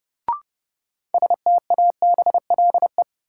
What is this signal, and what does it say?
Morse code, heard as a keyed tone, about 32 wpm, 700 Hz. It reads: STABLE
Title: Sentences from Top 1500 Words - New Word Drill - 30wpm